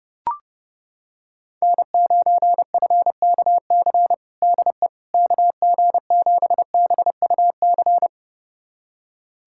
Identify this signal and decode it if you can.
Morse code, heard as a keyed tone, about 30 wpm, 700 Hz. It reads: N9FKC DE KG7BUC
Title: USA Callsign DE Callsign 30wpm